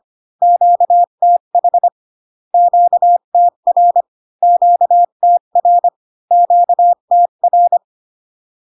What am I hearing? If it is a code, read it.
Morse code, heard as a keyed tone, about 25 wpm, 700 Hz. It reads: QTH QTR QTR QTR